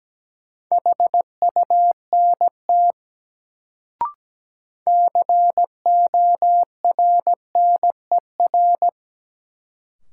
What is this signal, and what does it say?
Morse code, heard as a keyed tone, about 17 wpm, 700 Hz. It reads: HUNT CORNER
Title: Sentences from Top 1600 Words - New Word Drill - 17wpm